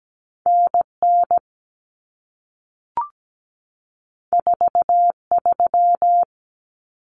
Morse code, heard as a keyed tone, about 17 wpm, 700 Hz. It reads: NN 43
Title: Common Cut Numbers - 2-Digit 17wpm